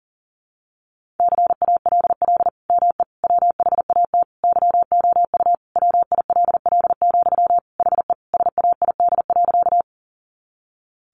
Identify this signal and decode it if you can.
Morse code, heard as a keyed tone, about 40 wpm, 700 Hz. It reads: CALL ME WHAT YOU WILL, HE SAID.
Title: Sentences from Top 200 Words 40wpm